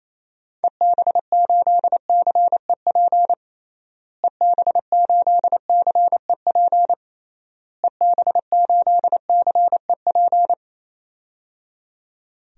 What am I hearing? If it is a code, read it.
Morse code, heard as a keyed tone, about 28 wpm, 700 Hz. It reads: EB8CEP EB8CEP EB8CEP
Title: DXCC-100 Call Signs - 6 Characters - Repeated 3x - 28wpm